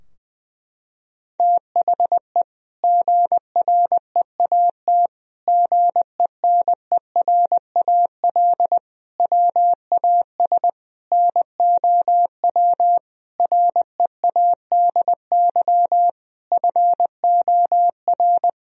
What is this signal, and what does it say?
Morse code, heard as a keyed tone, about 21 wpm, 700 Hz. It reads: THE GREAT GENERAL WAS NOW READY FOR
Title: Sentences from Top 1000 Words - Review Part 4 - 20wpm